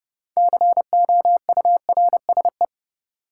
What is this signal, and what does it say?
Morse code, heard as a keyed tone, about 32 wpm, 700 Hz. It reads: COURSE
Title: Sentences from Top 500 Words - New Word Drill - 30wpm